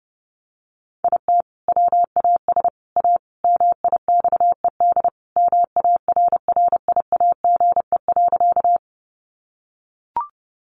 Morse code, heard as a keyed tone, about 30 wpm, 700 Hz. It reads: IT WAS A MIXED MARRIAGE.
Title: Sentences from Top 1500 Words 30wpm